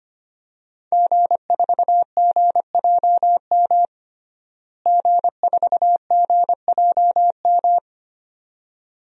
Morse code, heard as a keyed tone, about 25 wpm, 700 Hz. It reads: G4GJM G4GJM
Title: DXCC-100 Call Signs - 5 Characters - Repeated 2x - 25wpm